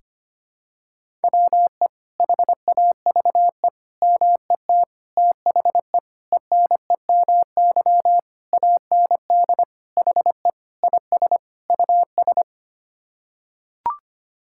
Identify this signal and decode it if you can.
Morse code, heard as a keyed tone, about 25 wpm, 700 Hz. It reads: WE HAVE MET THE ENEMY AND HE IS US
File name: Sets of 9 Words - 25wpm